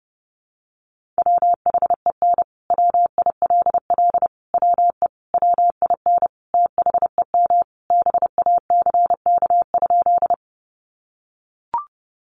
Morse code, heard as a keyed tone, about 30 wpm, 700 Hz. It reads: WHEN WILL WE WIN THEM BACK?
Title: Sentences from Top 1500 Words 30wpm